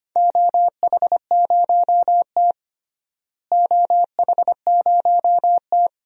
Morse code, heard as a keyed tone, about 25 wpm, 700 Hz. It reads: OH0T OH0T
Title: DXCC-100 Call Signs - 4 Characters - Repeated 2x - 25wpm